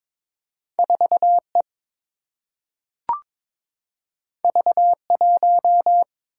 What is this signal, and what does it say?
Morse code, heard as a keyed tone, about 22 wpm, 700 Hz. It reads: 4 E V 1
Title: Single-Letter Single-Number 22wpm